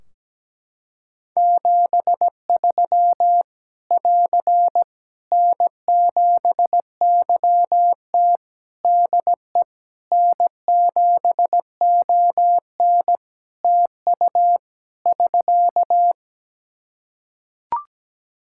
Morse code, heard as a keyed tone, about 17 wpm, 700 Hz. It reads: 73 <AR> N7YT DE N7ON TU <SK>
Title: Practice QSO - Exchange 4 - Ending - 17wpm